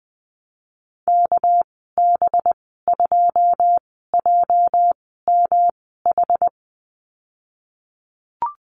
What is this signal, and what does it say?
Morse code, heard as a keyed tone, about 20 wpm, spaced as for 15 wpm, 700 Hz. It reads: KB2JMH
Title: 231 Lesson 37g - Letter x - Call Signs 20wpm